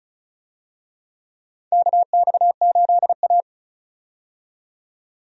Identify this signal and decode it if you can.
Morse code, heard as a keyed tone, about 36 wpm, 700 Hz. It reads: KX8A
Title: USA Call Signs - 4 Characters - 35wpm